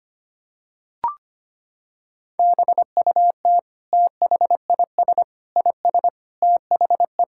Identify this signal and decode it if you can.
Morse code, heard as a keyed tone, about 25 wpm, 700 Hz. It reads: BUT THIS IS THE
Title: Sentences from Top 1000 Words - Review Part 5 - 25wpm